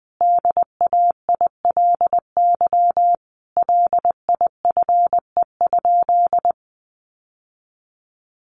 Morse code, heard as a keyed tone, about 20 wpm, 700 Hz. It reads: DAILY LIFE?
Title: Sentences from Top 1100 Words 20wpm